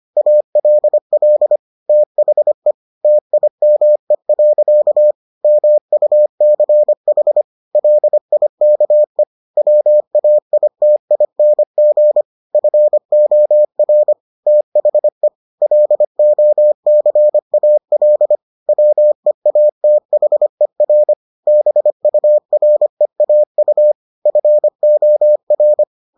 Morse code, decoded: ALL THE TIME. MUCH LIKE WAITING FOR THE LOCAL WEATHER BUREAU FOR

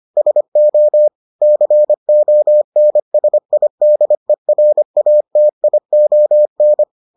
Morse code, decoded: SO CONSIDERATION

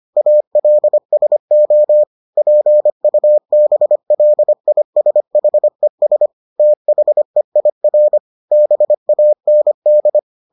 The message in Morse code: ALSO PUBLISHES THEIR BAND

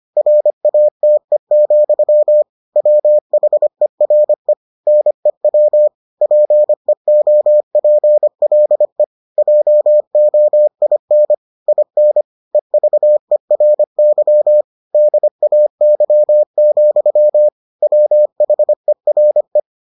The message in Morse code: RATE, WHERE NEW PEOPLE JOIN IN EVERY DAY, WHERE

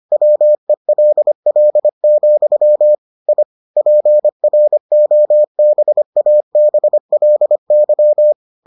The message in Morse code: WELL, I PROBABLY

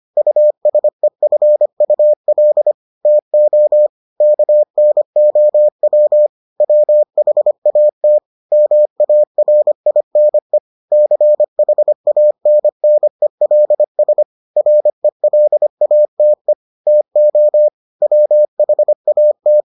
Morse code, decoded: USEFUL TO KNOW WHAT MARINE CHANNELS RELATE TO WHAT

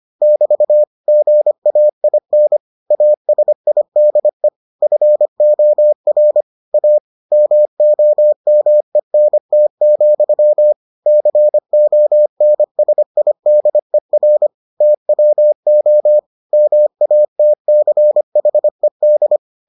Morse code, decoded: <BT> GAIN ASIDE FOR A MOMENT, CONSIDER TWO MATCHED

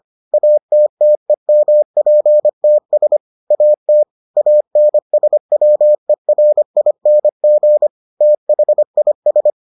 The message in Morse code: ATTEMPTS AT ANSWERING THIS

